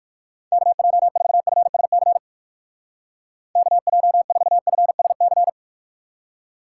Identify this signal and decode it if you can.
Morse code, heard as a keyed tone, about 49 wpm, 700 Hz. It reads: KJ4FSC KJ4FSC
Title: Speed Racing - USA 2x3 Call Signs 45wpm